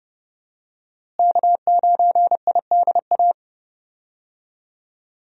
Morse code, heard as a keyed tone, about 32 wpm, 700 Hz. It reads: K9IDA
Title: USA Call Signs - 5 Characters - Repeated 3x - 30wpm